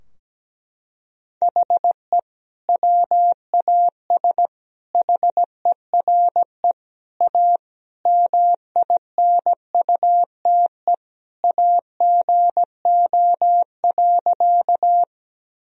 Morse code, heard as a keyed tone, about 17 wpm, 700 Hz. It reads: HE WAS HERE A MINUTE AGO.
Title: Sentences from Top 1000 Words - Review Part 3 - 17wpm